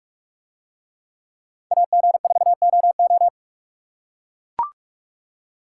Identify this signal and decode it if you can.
Morse code, heard as a keyed tone, about 44 wpm, 700 Hz. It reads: AG4OO